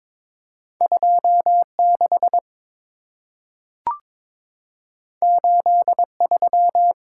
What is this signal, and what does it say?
Morse code, heard as a keyed tone, about 23 wpm, 690 Hz. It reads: 26 83
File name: practice / 22wpm / Two Letter-Number Combinations - 22wpm